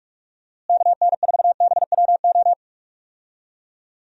Morse code, heard as a keyed tone, about 42 wpm, 700 Hz. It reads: KN4BWO
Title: USA Call Signs - 6 Characters - Repeated 2x - 45wpm